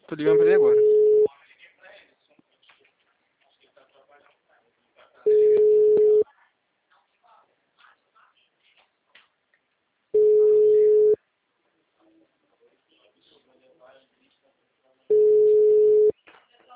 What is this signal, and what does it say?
Ringback tone (Europe, 425 Hz), with speech.